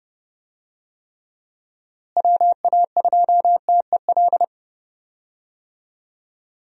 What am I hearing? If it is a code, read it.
Morse code, heard as a keyed tone, about 32 wpm, 700 Hz. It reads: WA2TEL
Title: USA Call Signs - 6 Characters - Repeated 2x - 30wpm